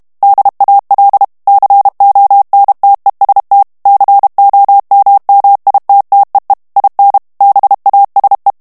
Morse code, heard as a keyed tone, about 32 wpm, 800 Hz. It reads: DAL CONTEST COMMITTEE IN BASE
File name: ITALIEN texte clair 2 vitesse 30 mots minute